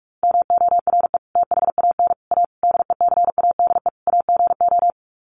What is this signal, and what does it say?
Morse code, heard as a keyed tone, about 44 wpm, 700 Hz. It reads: MORE THAN A DECADE AGO
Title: Sets of 5 Words Encore2 45wpm